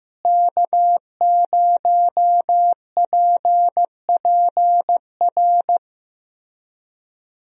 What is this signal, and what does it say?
Morse code, heard as a keyed tone, about 15 wpm, 700 Hz. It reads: K0PPR
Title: USA Call Signs - 5 Characters - Repeated 3x - 15wpm